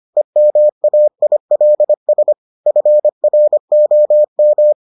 Morse code, heard as a keyed tone, about 25 wpm, 600 Hz. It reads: EMAILS FROM